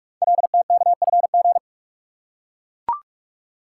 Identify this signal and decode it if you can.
Morse code, heard as a keyed tone, about 44 wpm, 700 Hz. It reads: LTKFG
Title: Code Groups - 5 Letters - 45wpm